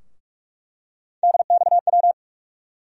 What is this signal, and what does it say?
Morse code, heard as a keyed tone, about 46 wpm, 700 Hz. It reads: DXW
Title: USA Call Sign Postfixes - 3 Letters 45wpm